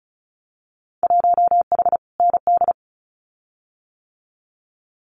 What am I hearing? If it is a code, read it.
Morse code, heard as a keyed tone, about 34 wpm, 700 Hz. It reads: 1H ND
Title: Winter Field Day Exchanges 35wpm